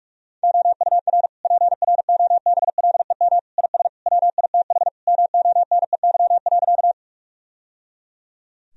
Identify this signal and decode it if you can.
Morse code, heard as a keyed tone, about 46 wpm, 700 Hz. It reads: OUR PROBLEM IS WITH MONEY.
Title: Sentences from Top 1000 Words - Review Part 2 - 45wpm